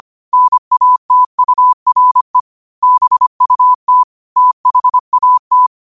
Morse code, heard as a keyed tone, about 25 wpm, 1000 Hz. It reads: NATURE BUT THAT
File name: qotdfiles / 25wpm